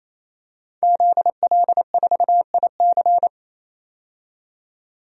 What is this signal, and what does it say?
Morse code, heard as a keyed tone, about 28 wpm, 700 Hz. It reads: ZL4IC
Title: DXCC-100 Call Signs - 5 Characters - 28wpm